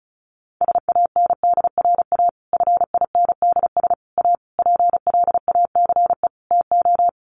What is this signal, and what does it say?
Morse code, heard as a keyed tone, about 34 wpm, 700 Hz. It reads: SANDRA FINDS A PLACE TO
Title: Easy Phrases 35wpm